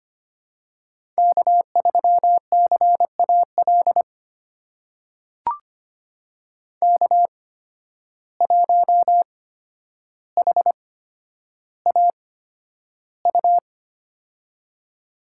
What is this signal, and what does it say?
Morse code, heard as a keyed tone, about 25 wpm, spaced as for 11 wpm, 700 Hz. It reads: K3CAL K1HAU
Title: USA Call Signs - 5 Characters - 8x Character Spacing - 25wpm